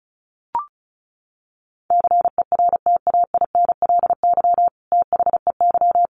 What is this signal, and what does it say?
Morse code, heard as a keyed tone, about 34 wpm, 700 Hz. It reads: CERTAINLY THEY
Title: Sentences from Top 900 Words 35wpm